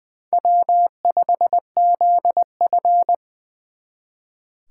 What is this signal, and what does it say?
Morse code, heard as a keyed tone, about 24 wpm, 700 Hz. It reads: W5ZF
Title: Speed Racing - USA 1x2 Call Signs 20wpm